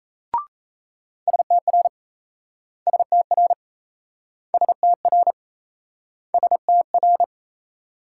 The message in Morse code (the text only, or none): STR STR STR STR